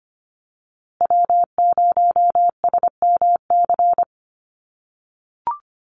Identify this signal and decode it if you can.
Morse code, heard as a keyed tone, about 25 wpm, 700 Hz. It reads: W0SMC
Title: USA Call Signs Repeated 2x 25wpm